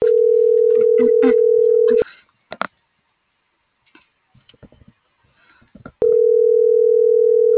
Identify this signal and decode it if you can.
Ringback tone (North America), 440 + 480 Hz.